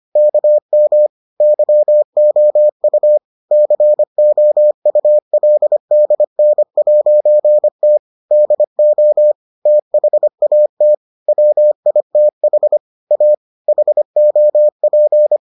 Morse code, heard as a keyed tone, about 25 wpm, 600 Hz. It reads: KM YOU COULDN'T DO THAT WITH A HOP